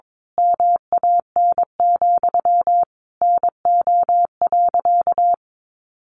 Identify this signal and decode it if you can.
Morse code, heard as a keyed tone, about 22 wpm, 700 Hz. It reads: MAN, NO.
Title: A Sound Of Thunder 22wpm